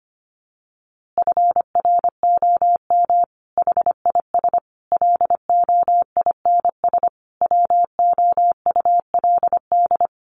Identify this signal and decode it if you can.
Morse code, heard as a keyed tone, about 25 wpm, 700 Hz. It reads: FROM HIS LOINS WOULD
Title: A Sound Of Thunder 25wpm